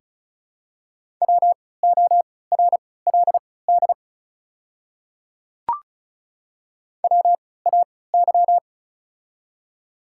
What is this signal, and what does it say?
Morse code, heard as a keyed tone, about 36 wpm, spaced as for 20 wpm, 700 Hz. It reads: WORLD WAY
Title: 3x Character Spacing - Top 300 Words - 35wpm